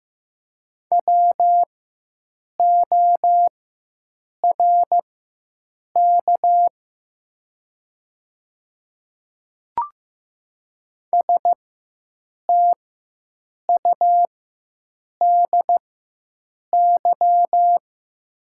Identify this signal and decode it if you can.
Morse code, heard as a keyed tone, about 15 wpm, spaced as for 7 wpm, 700 Hz. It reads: WORK STUDY
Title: 4x Character Spacing - Top 300 Words - 15wpm